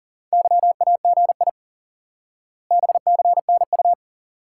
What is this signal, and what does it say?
Morse code, heard as a keyed tone, about 40 wpm, 700 Hz. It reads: YAGI BCNU
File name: IQE Course 3 - Lesson 89 - YAGI - Two Elements - 40wpm